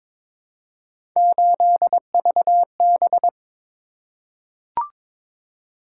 Morse code, heard as a keyed tone, about 22 wpm, 700 Hz. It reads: 8VB
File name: practice / 22wpm / Three Letter-Number Combinations - 22wpm